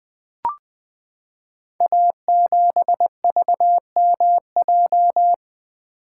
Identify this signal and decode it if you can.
Morse code, heard as a keyed tone, about 20 wpm, 700 Hz. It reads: A7VMJ
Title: Code Groups - 5 Letter:Numbers - 20wpm